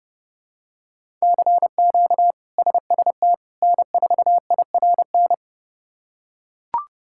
Morse code, heard as a keyed tone, about 30 wpm, 700 Hz. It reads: CQ SST N4IRN
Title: K1USN SST QSO Practice - 30wpm